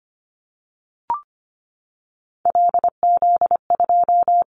Morse code, heard as a keyed tone, about 25 wpm, 700 Hz. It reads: LZ2
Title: Callsign Prefixes for Top 25 DXCC - 25wpm